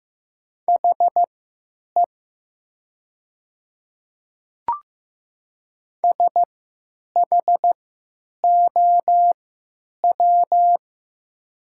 Morse code, heard as a keyed tone, about 15 wpm, spaced as for 9 wpm, 700 Hz. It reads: HE SHOW
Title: 3x Character Spacing - Top 300 Words - 15wpm